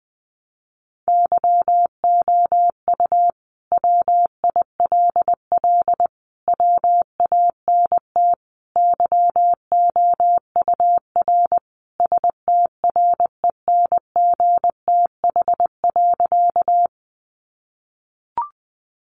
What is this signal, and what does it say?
Morse code, heard as a keyed tone, about 20 wpm, 700 Hz. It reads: YOU WILL WANT YOUR STRENGTH.
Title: Sentences from Top 1300 Words 20wpm